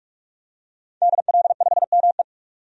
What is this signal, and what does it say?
Morse code, heard as a keyed tone, about 46 wpm, 700 Hz. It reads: DL5ME
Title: DXCC-100 Call Signs - 5 Characters - 45wpm